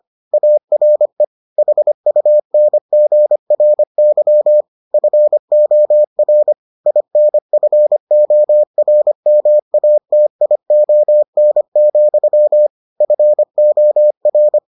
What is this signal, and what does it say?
Morse code, heard as a keyed tone, about 25 wpm, 600 Hz. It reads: ARE HUNGRY FOR INFORMATION, FOR